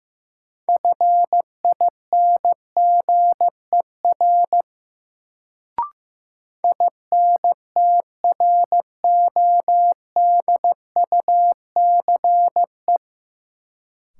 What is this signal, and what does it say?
Morse code, heard as a keyed tone, about 15 wpm, 700 Hz. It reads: FINGER INTRODUCE
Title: Sentences from Top 1600 Words - New Word Drill - 15wpm